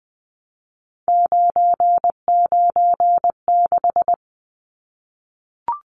Morse code, heard as a keyed tone, about 20 wpm, 700 Hz. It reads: 996
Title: Numbers - 3-Digit 20wpm